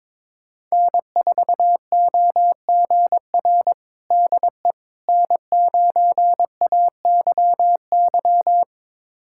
Morse code, heard as a keyed tone, about 22 wpm, 700 Hz. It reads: N4OGR DE N9AYY
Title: USA Callsign DE Callsign - 5 Characters - 22wpm